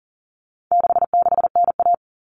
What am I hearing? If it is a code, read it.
Morse code, heard as a keyed tone, about 40 wpm, 700 Hz. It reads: 66NA
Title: All Cut Numbers - 4-Digit 40wpm